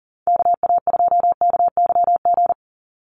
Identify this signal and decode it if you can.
Morse code, heard as a keyed tone, about 40 wpm, 700 Hz. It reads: KA2KYG
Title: USA Call Signs 40wpm